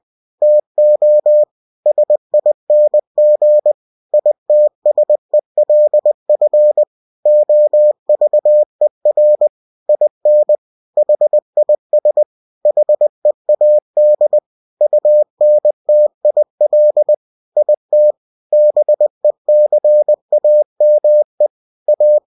Morse code, heard as a keyed tone, about 20 wpm, 600 Hz. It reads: TO SING ITSELF OVER IN HIS HEAD UNTIL IT BECAME A
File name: cw-books / winnie-the-pooh / winnie-20wpm-page-0103